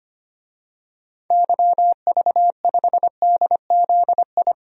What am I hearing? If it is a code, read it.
Morse code, heard as a keyed tone, about 25 wpm, 700 Hz. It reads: YV5DZI
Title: DXCC-100 Call Signs - 6 Characters - 25wpm